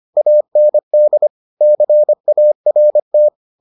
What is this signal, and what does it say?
Morse code, heard as a keyed tone, about 25 wpm, 600 Hz. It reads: AND CART